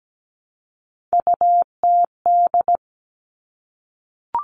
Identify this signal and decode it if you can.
Morse code, heard as a keyed tone, about 17 wpm, 700 Hz. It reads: UTD
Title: Three Challenging Letter-Number Combinations 17wpm